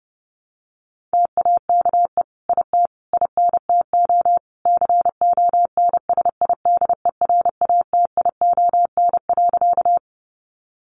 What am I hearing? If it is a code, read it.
Morse code, heard as a keyed tone, about 30 wpm, 700 Hz. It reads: TAKE IT INTO CONSIDERATION.
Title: Sentences from Top 1500 Words 30wpm